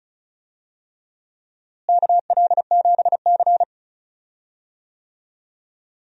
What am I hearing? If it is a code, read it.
Morse code, heard as a keyed tone, about 34 wpm, 700 Hz. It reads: KL7C
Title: DXCC-100 Call Signs - 4 Characters - Repeated 3x - 35wpm